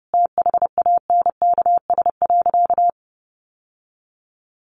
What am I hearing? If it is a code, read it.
Morse code, heard as a keyed tone, about 30 wpm, 700 Hz. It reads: THANKS.
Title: Sentences from Top 600 Words 30wpm